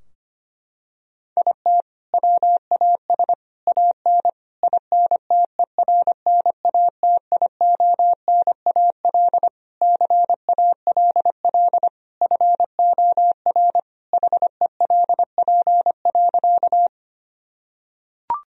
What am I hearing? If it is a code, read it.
Morse code, heard as a keyed tone, about 25 wpm, 700 Hz. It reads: IT WAS AN INTERNATIONAL CALL FOR HELP.
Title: Sentences from Top 1000 Words - Review Part 4 - 25wpm